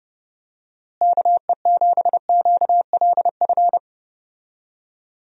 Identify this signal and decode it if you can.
Morse code, heard as a keyed tone, about 32 wpm, 700 Hz. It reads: KE7QLF
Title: USA Call Signs - 6 Characters - Repeated 2x - 30wpm